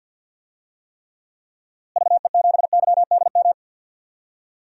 Morse code, heard as a keyed tone, about 52 wpm, 700 Hz. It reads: VE7YDM
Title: DXCC-100 Call Signs - 6 Characters - Repeated 2x - 50wpm